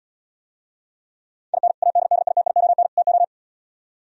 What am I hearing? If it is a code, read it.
Morse code, heard as a keyed tone, about 77 wpm, 700 Hz. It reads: IN ADDITION TO